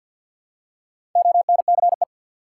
Morse code, heard as a keyed tone, about 51 wpm, 700 Hz. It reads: ONCE
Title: Four Letter Words - 50wpm